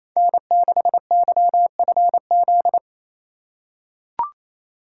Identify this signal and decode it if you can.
Morse code, heard as a keyed tone, about 28 wpm, 700 Hz. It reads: N6YFZ
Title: USA Call Signs - 2x Character Spacing - 28wpm